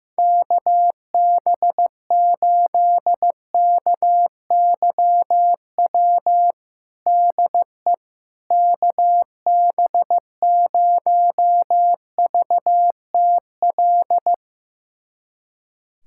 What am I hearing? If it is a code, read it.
Morse code, heard as a keyed tone, about 15 wpm, 700 Hz. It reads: KB8KYW DE KB0VTL
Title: USA Callsign DE Callsign 15wpm